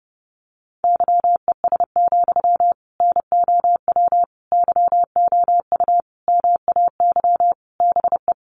Morse code, heard as a keyed tone, about 30 wpm, 700 Hz. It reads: YES, NOW YOU MAY BE
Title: Sentences from Top 800 Words 30wpm